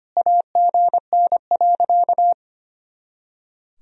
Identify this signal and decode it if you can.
Morse code, heard as a keyed tone, about 25 wpm, 700 Hz. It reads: AGN.
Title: Sentences from Top 500 Words and Abbreviations 25wpm